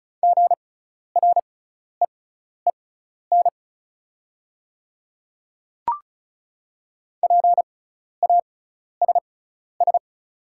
Morse code, heard as a keyed tone, about 36 wpm, spaced as for 12 wpm, 700 Hz. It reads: GREEN PASS
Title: 6x Character Spacing - Top 500 Words - 35wpm